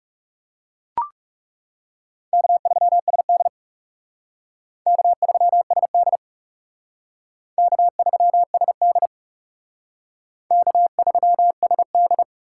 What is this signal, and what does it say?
Morse code, heard as a keyed tone, about 37 wpm, 700 Hz. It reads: K3SD K3SD K3SD K3SD
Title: Speed Racing - USA 1x2 Call Signs 30wpm